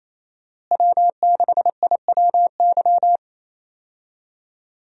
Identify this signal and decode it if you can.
Morse code, heard as a keyed tone, about 28 wpm, 700 Hz. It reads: W6IWY
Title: USA Call Signs - 5 Characters - Repeated 2x - 28wpm